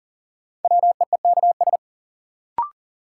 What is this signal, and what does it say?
Morse code, heard as a keyed tone, about 37 wpm, 700 Hz. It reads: WEEKS
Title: Sets of 5 Words - New Word Drill - 40wpm